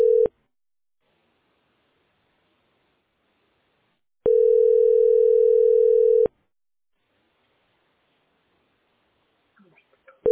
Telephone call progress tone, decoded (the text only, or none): ringback tone